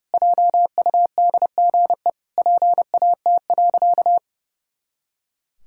Morse code, heard as a keyed tone, about 30 wpm, 700 Hz. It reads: JUDGE PAT.
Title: Pangram Challenge v2 - 30wpm